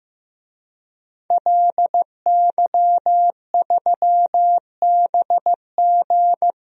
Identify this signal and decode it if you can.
Morse code, heard as a keyed tone, about 15 wpm, 700 Hz. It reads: LY3BG
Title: DXCC-100 Call Signs - 5 Characters - Repeated 2x - 15wpm